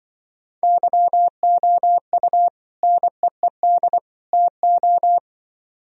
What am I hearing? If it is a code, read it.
Morse code, heard as a keyed tone, about 25 wpm, 700 Hz. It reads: YOU NEED TO
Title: Speed-Racing - Sets of 3 Words 20wpm